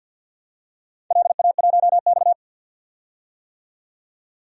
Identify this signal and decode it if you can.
Morse code, heard as a keyed tone, about 51 wpm, 700 Hz. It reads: LA1X
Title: DXCC-100 Call Signs - 4 Characters - 50wpm